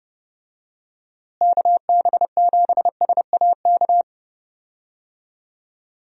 Morse code, heard as a keyed tone, about 30 wpm, 700 Hz. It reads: KB7SAK
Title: USA Call Signs - 6 Characters - Repeated 2x - 30wpm